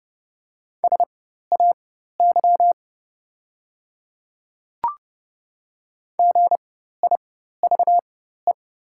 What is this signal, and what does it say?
Morse code, heard as a keyed tone, about 30 wpm, spaced as for 14 wpm, 690 Hz. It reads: SAY GIVE
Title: 4x Character Spacing - Top 200 Words - 30wpm